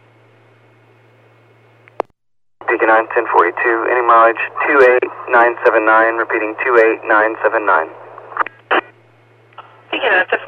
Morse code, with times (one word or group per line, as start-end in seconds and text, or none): none